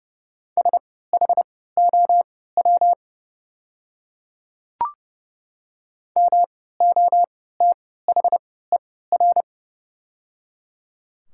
Morse code, heard as a keyed tone, about 30 wpm, spaced as for 17 wpm, 700 Hz. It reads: SHOW MOTHER
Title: 3x Character Spacing - Top 300 Words - 30wpm